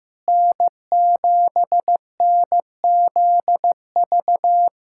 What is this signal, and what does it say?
Morse code, heard as a keyed tone, about 16 wpm, 700 Hz. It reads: N7NZV
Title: USA Callsign DE Callsign 15wpm